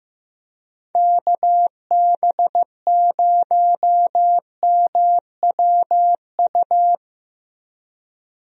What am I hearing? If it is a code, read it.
Morse code, heard as a keyed tone, about 15 wpm, 700 Hz. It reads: KB0MWU
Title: USA Call Signs - 6 Characters - Repeated 3x - 15wpm